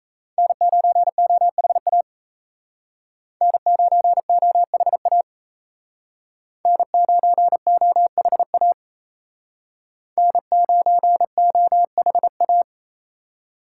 Morse code, heard as a keyed tone, about 35 wpm, 700 Hz. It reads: N9OHA N9OHA N9OHA N9OHA